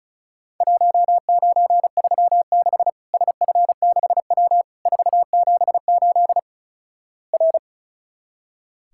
Morse code, heard as a keyed tone, about 36 wpm, 700 Hz. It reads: 1936 SF6W 478 R
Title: Worked All Europe DX Contest - QTC Exchange - 35wpm